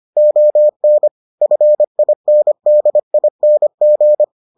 Morse code, heard as a keyed tone, about 25 wpm, 600 Hz. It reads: ON FINDING